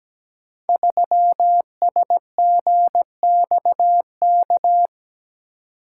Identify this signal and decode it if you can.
Morse code, heard as a keyed tone, about 17 wpm, 700 Hz. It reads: 3SGXK
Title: Code Groups - 5 Letter:Numbers - 17wpm